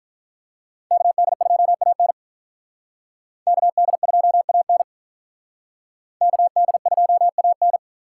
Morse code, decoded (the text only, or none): KD2AN KD2AN KD2AN